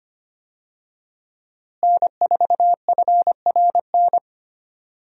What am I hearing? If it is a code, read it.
Morse code, heard as a keyed tone, about 25 wpm, 700 Hz. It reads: N4FRN
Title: USA Call Signs - 5 Characters - Repeated 3x - 25wpm